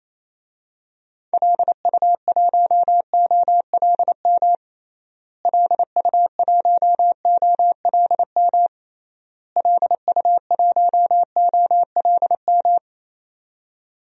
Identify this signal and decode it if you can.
Morse code, heard as a keyed tone, about 28 wpm, 700 Hz. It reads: LU1OLM LU1OLM LU1OLM
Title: DXCC-100 Call Signs - 6 Characters - Repeated 3x - 28wpm